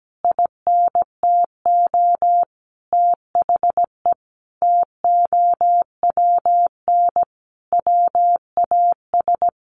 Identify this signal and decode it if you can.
Morse code, heard as a keyed tone, about 17 wpm, 700 Hz. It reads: INTO THE TOWN WAS
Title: Sentences from Top 1500 Words 17wpm